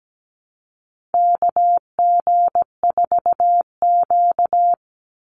Morse code, heard as a keyed tone, about 17 wpm, 700 Hz. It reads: KG4Q
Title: Call Signs - 17wpm